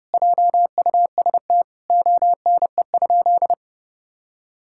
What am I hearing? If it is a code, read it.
Morse code, heard as a keyed tone, about 28 wpm, 700 Hz. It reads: JUST ONE?
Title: Sentences from Top 1000 Words - Review Part 6 - 30wpm